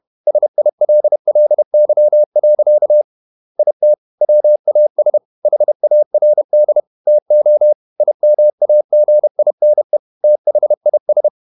Morse code, heard as a keyed tone, about 31 wpm, 600 Hz. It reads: SILLY. IT WAS HARD TO IMAGINE THIS
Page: Never 31wpm | CW med Gnister